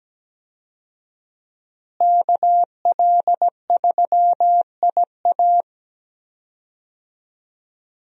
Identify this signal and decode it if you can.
Morse code, heard as a keyed tone, about 17 wpm, 700 Hz. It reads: KL3IA
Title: USA Call Signs - 5 Characters - Repeated 3x - 17wpm